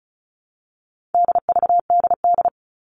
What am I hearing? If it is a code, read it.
Morse code, heard as a keyed tone, about 34 wpm, 700 Hz. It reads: DVDD
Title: All Cut Numbers - 4-Digit 35wpm